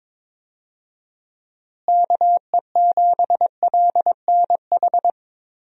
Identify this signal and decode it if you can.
Morse code, heard as a keyed tone, about 22 wpm, 700 Hz. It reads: KE7LNH
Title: USA Call Signs - 6 Characters - Repeated 3x - 22wpm